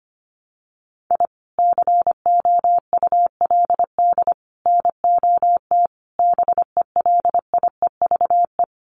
Morse code, heard as a keyed tone, about 25 wpm, 700 Hz. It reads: I COULD NOT BELIEVE
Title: Sets of 5 Words Encore 25wpm